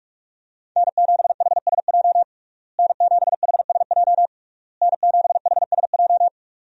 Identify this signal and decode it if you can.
Morse code, heard as a keyed tone, about 43 wpm, 700 Hz. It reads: N7HSJ N7HSJ N7HSJ
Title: USA Call Signs - 5 Characters - Repeated 3x - 45wpm